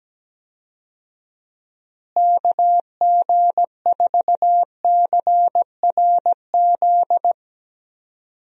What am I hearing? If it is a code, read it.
Morse code, heard as a keyed tone, about 17 wpm, 700 Hz. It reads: KG4CRZ
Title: USA Call Signs - 6 Characters - Repeated 2x - 17wpm